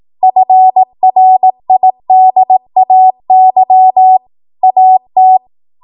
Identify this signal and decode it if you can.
Morse code, heard as a keyed tone, about 18 wpm, 750 Hz. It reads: FRIDAY AT
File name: Archive / 18 WPM / 18WPM